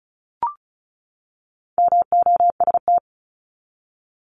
Morse code, heard as a keyed tone, about 34 wpm, 700 Hz. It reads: MOST
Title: LY Suffix - Top 100 Words with Base Words 35wpm